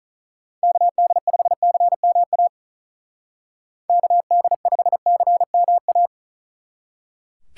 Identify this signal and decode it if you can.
Morse code, heard as a keyed tone, about 44 wpm, 700 Hz. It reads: KD5CMA KD5CMA